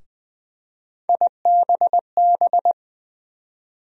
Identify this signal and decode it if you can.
Morse code, heard as a keyed tone, about 20 wpm, 700 Hz. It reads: IBB
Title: USA Call Sign Postfixes - 3 Letters 20wpm